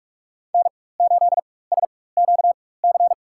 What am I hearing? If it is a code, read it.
Morse code, heard as a keyed tone, about 45 wpm, spaced as for 21 wpm, 700 Hz. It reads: N8SQC
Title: USA Call Signs - 4x Character Spacing - 45wpm